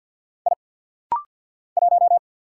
Morse code, heard as a keyed tone, about 50 wpm, 690 Hz. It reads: I 1
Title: Single Letter-Number - Rapid-Fire 50wpm